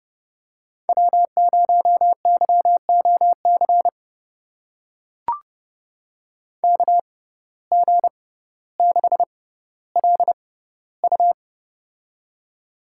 Morse code, heard as a keyed tone, about 32 wpm, spaced as for 16 wpm, 700 Hz. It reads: W0YOC KG6LU
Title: USA Call Signs - 5 Characters - 6x Character Spacing - 30wpm